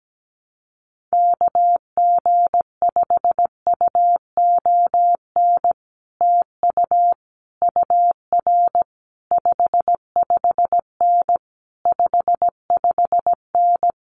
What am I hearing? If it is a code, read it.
Morse code, heard as a keyed tone, about 17 wpm, 700 Hz. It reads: KG5UON TU UR 55N 55N
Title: Common QSO Phrases Encore 17wpm